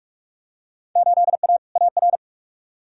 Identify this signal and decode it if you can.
Morse code, heard as a keyed tone, about 43 wpm, 700 Hz. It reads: 8A AR
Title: Field Day Exchanges - 45wpm